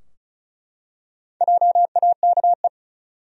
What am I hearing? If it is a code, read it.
Morse code, heard as a keyed tone, about 36 wpm, 700 Hz. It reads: JAKE
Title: USA Common Names - Max 4 Letters - 35wpm